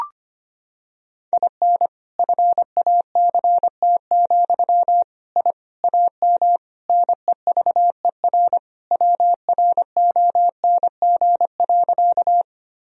Morse code, heard as a keyed tone, about 25 wpm, 700 Hz. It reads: IN FACT, I AM NEVER WRONG.
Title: Sentences from Top 1000 Words - Review Part 3 - 25wpm